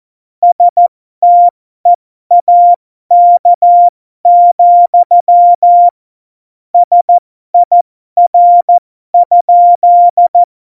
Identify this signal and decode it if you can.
Morse code, heard as a keyed tone, about 14 wpm, spaced as for 12 wpm, 700 Hz. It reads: STEAK, SIR?
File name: Morsekurs / Witze